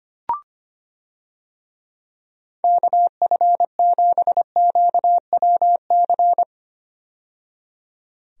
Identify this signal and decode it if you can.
Morse code, heard as a keyed tone, about 25 wpm, 700 Hz. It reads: KF7QWC
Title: USA Call Signs - 6 Characters - 25wpm